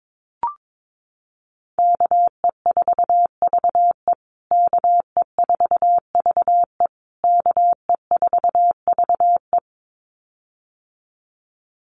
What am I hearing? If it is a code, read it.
Morse code, heard as a keyed tone, about 22 wpm, 700 Hz. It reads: KE4VE KE4VE KE4VE
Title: USA Call Signs Repeated 3x 22wpm